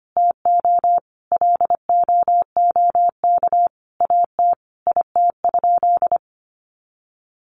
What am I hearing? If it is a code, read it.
Morse code, heard as a keyed tone, about 25 wpm, 700 Hz. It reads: TO LOOK AT IT?
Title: Sentences from Top 100 Words 25wpm